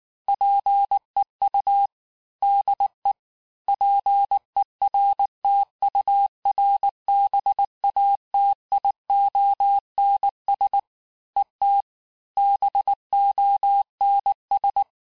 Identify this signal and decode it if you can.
Morse code, heard as a keyed tone, about 19 wpm, 800 Hz. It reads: PEU DE PERTURBATIONS ET BONS
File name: cours-du-lundi files / cours-1589-texte-FIN-de-QSO-vitesse-18mpm